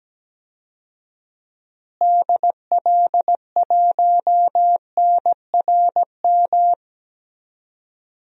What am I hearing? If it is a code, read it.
Morse code, heard as a keyed tone, about 17 wpm, 700 Hz. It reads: DL1NRM
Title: DXCC-100 Call Signs - 6 Characters - Repeated 2x - 17wpm